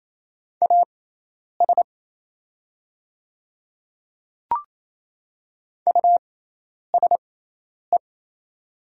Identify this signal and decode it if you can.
Morse code, heard as a keyed tone, about 28 wpm, spaced as for 10 wpm, 700 Hz. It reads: AS USE
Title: 6x Character Spacing - Top 200 Words - 28wpm